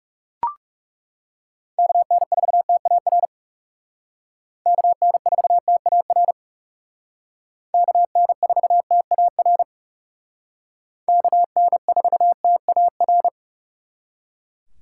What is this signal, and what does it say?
Morse code, heard as a keyed tone, about 38 wpm, 700 Hz. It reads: KN4TAR KN4TAR KN4TAR KN4TAR